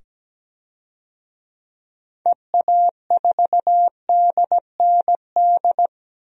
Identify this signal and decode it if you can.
Morse code, heard as a keyed tone, about 17 wpm, 700 Hz. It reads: EA4DND
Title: DXCC-100 Call Signs - 6 Characters - 17wpm